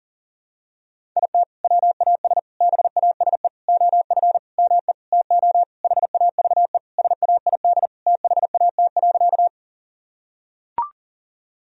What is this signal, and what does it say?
Morse code, heard as a keyed tone, about 40 wpm, 700 Hz. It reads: IT WAS BASE OF ME TO HAVE SAID THAT.
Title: Sentences from Top 1000 Words - Review Part 1 - 40wpm